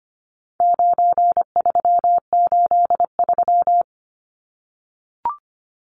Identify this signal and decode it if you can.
Morse code, heard as a keyed tone, about 25 wpm, 700 Hz. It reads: 9383
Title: Numbers - 4-Digit 25wpm